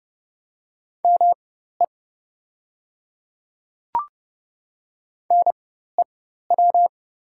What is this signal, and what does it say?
Morse code, heard as a keyed tone, about 30 wpm, spaced as for 14 wpm, 700 Hz. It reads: ME NEW
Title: 4x Character Spacing - Top 200 Words - 30wpm